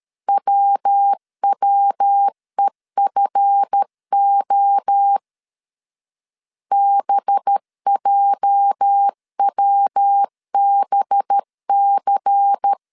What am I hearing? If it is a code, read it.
Morse code, heard as a keyed tone, about 13 wpm, 800 Hz. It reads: WWEFO BJWBC
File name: CW / DET / 1 9 1500C S A 12wpm 3800